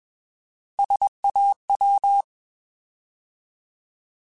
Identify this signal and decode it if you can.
Morse code, heard as a keyed tone, about 21 wpm, 800 Hz. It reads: SAW
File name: Alphabetical / Alphabetical mp3s / Ss @20